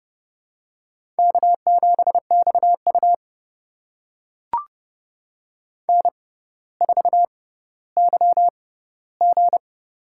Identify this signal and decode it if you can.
Morse code, heard as a keyed tone, about 30 wpm, spaced as for 15 wpm, 700 Hz. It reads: K7XU N4YG
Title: USA Call Signs - 4 Characters - 6x Character Spacing - 30wpm